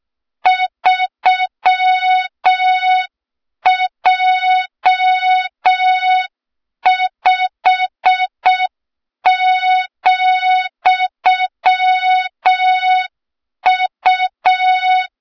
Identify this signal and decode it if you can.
Morse code, heard as a keyed tone, about 6 wpm, 3700 Hz, with noise. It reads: 3J5,U
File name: Morze-HA2MN / hang / 23 GYAKORLAT